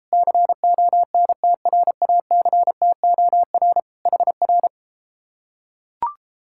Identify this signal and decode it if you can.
Morse code, heard as a keyed tone, about 34 wpm, 700 Hz. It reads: CONTRACTOR HR